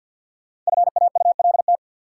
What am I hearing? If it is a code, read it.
Morse code, heard as a keyed tone, about 53 wpm, 700 Hz. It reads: FAULT
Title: Five Letter Words - 50wpm